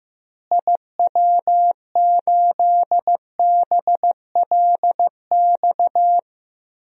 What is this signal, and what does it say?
Morse code, heard as a keyed tone, about 15 wpm, 700 Hz. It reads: IW8BLX
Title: DXCC-100 Call Signs - 6 Characters - Repeated 3x - 15wpm